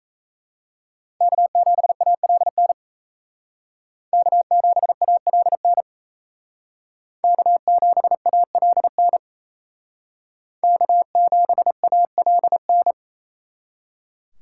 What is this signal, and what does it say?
Morse code, heard as a keyed tone, about 35 wpm, 700 Hz. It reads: K7ALN K7ALN K7ALN K7ALN
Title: Speed Racing - USA 1x3 Call Signs 28wpm